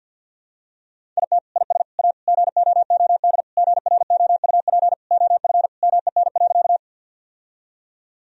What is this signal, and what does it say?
Morse code, heard as a keyed tone, about 50 wpm, 700 Hz. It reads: IT IS A GOOD GROUP OF MEN.
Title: Sentences from Top 1000 Words - Review Part 1 - 50wpm